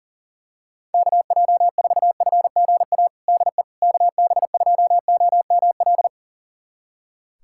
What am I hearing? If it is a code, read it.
Morse code, heard as a keyed tone, about 40 wpm, 700 Hz. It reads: KJ4FGA DE KB2OML
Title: USA Callsign DE Callsign - 6 Characters - 40wpm